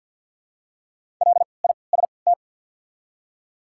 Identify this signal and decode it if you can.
Morse code, heard as a keyed tone, about 47 wpm, spaced as for 28 wpm, 690 Hz. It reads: LIST
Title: 3x Character Spacing - Top 500 Words - 50wpm